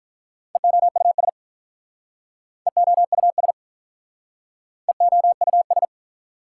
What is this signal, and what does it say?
Morse code, heard as a keyed tone, about 42 wpm, 700 Hz. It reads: EOUS EOUS EOUS
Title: Speed-Racing Prefixes and Suffixes 35wpm